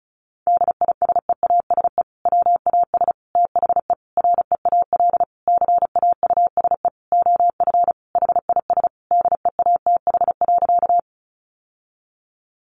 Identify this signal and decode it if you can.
Morse code, heard as a keyed tone, about 34 wpm, 700 Hz. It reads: DISEASE WAS THE REAL CAUSE OF HIS DEATH.
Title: Sentences from Top 1100 Words 35wpm